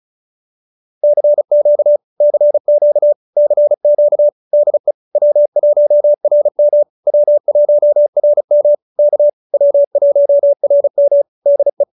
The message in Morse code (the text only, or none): CQ CQ CQ DE W1RM W1RM K W1RM DE